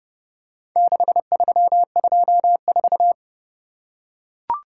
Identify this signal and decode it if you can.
Morse code, heard as a keyed tone, about 32 wpm, 700 Hz. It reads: 6324